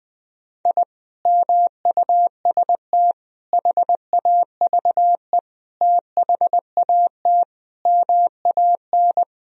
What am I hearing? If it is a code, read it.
Morse code, heard as a keyed tone, about 20 wpm, 700 Hz. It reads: I MUST HAVE THAT MAN
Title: Sentences from Top 1000 Words - Review Part 2 - 20wpm